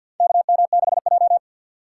Morse code, heard as a keyed tone, about 52 wpm, 700 Hz. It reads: KM6J
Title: USA Call Signs - 4 Characters - 6x Character Spacing - 50wpm